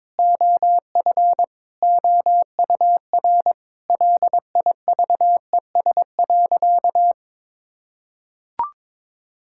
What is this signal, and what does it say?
Morse code, heard as a keyed tone, about 22 wpm, 700 Hz. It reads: OF OUR LIVES.
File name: Difficult Sentences with Hints - 22wpm